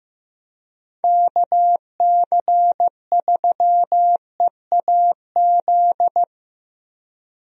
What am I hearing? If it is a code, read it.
Morse code, heard as a keyed tone, about 15 wpm, 700 Hz. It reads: KC3EAZ
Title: USA Call Signs - 6 Characters - 15wpm